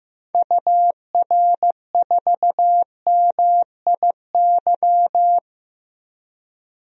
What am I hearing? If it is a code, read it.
Morse code, heard as a keyed tone, about 15 wpm, 700 Hz. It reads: UR4MIY
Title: DXCC-100 Call Signs - 6 Characters - 15wpm